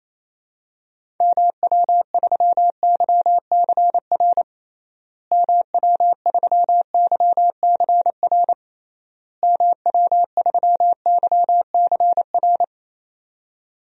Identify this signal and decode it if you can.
Morse code, heard as a keyed tone, about 28 wpm, 700 Hz. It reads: MW3YCR MW3YCR MW3YCR
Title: DXCC-100 Call Signs - 6 Characters - Repeated 3x - 28wpm